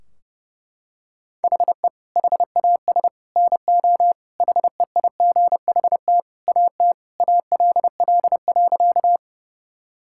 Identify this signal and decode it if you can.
Morse code, heard as a keyed tone, about 30 wpm, 700 Hz. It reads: HE HAS NO HEIGHT AT ALL.
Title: Sentences from Top 1600 Words 30wpm